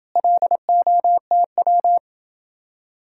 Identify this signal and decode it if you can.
Morse code, heard as a keyed tone, about 28 wpm, 700 Hz. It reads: LOTW